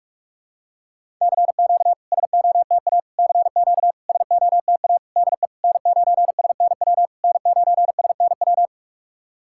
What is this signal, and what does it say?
Morse code, heard as a keyed tone, about 48 wpm, 690 Hz. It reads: CQ SOTA CQ SOTA DE N9SNW N9SNW
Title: SOTA QSO Practice 45wpm